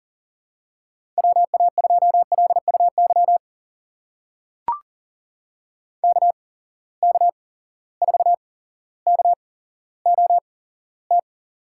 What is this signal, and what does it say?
Morse code, heard as a keyed tone, about 39 wpm, spaced as for 17 wpm, 700 Hz. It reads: WA2LUY KK4KOT
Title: USA Call Signs - 6 Characters - 8x Character Spacing - 40wpm